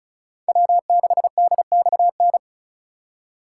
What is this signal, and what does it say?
Morse code, heard as a keyed tone, about 45 wpm, 700 Hz. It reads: W6DXN